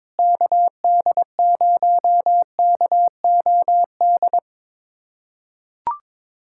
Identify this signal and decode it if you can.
Morse code, heard as a keyed tone, about 22 wpm, 700 Hz. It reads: KD0KOD
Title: USA Callsign DE Callsign - 6 Characters - 22wpm